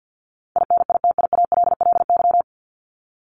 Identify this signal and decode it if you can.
Morse code, heard as a keyed tone, about 50 wpm, 700 Hz. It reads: INITIALLY
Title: Top 1251-1500 Words 50wpm